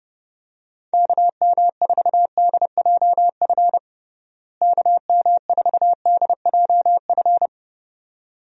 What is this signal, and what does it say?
Morse code, heard as a keyed tone, about 30 wpm, 700 Hz. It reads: KM4DJF KM4DJF
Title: USA Call Signs - 6 Characters - Repeated 2x - 30wpm